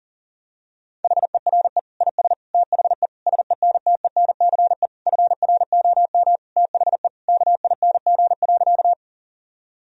Morse code, heard as a keyed tone, about 40 wpm, 700 Hz. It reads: HERE IS THE SENTENCE FROM THE KING.
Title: Sentences from Top 1000 Words - Review Part 1 - 40wpm